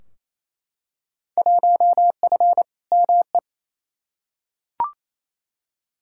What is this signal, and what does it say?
Morse code, heard as a keyed tone, about 28 wpm, 700 Hz. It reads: 1F ME
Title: Field Day Exchanges Encore - 28wpm